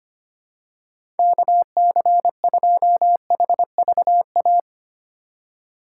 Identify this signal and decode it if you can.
Morse code, heard as a keyed tone, about 25 wpm, 700 Hz. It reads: KC2HVA
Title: USA Call Signs - 6 Characters - 2x Character Spacing - 25wpm